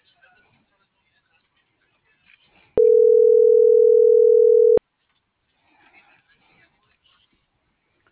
Ringback tone (North America), 440 + 480 Hz.